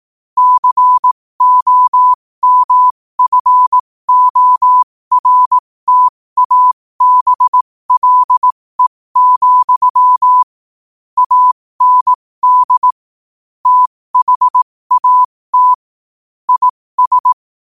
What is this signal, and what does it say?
Morse code, heard as a keyed tone, about 18 wpm, spaced as for 15 wpm, 1000 Hz. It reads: COMFORTABLE, AND THAT IS
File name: qotdfiles / 15wpm